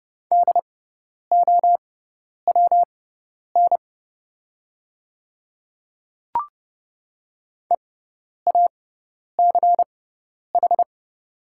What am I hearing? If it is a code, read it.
Morse code, heard as a keyed tone, about 29 wpm, spaced as for 10 wpm, 690 Hz. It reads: DOWN EACH
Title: 6x Character Spacing - Top 200 Words - 30wpm